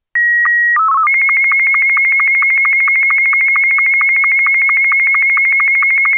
SSTV (slow-scan television), Martin 2.
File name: sstv-essexham-image01-martin2